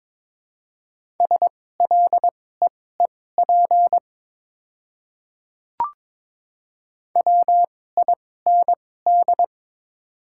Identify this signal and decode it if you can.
Morse code, heard as a keyed tone, about 22 wpm, spaced as for 16 wpm, 700 Hz. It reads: SLEEP WIND
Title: 2x Character Spacing - Top 500 Words - 22wpm